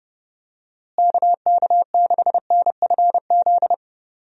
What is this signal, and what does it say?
Morse code, heard as a keyed tone, about 30 wpm, 700 Hz. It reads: KK6NFZ
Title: USA Call Signs - 6 Characters - Repeated 2x - 30wpm